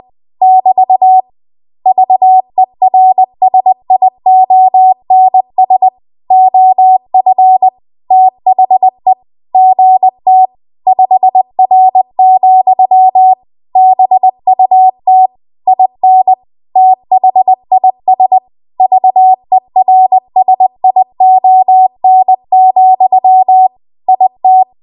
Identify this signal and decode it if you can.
Morse code, heard as a keyed tone, about 20 wpm, 750 Hz. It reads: <BT> VERSIONS OF THE GT 5R, BUT IN THIS VERSION, IT
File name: Archive / 20 WPM / 20WPM